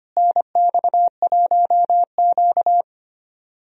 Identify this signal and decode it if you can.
Morse code, heard as a keyed tone, about 25 wpm, 700 Hz. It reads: NX1Q
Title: USA Call Signs - 4 Characters - Repeated 3x - 25wpm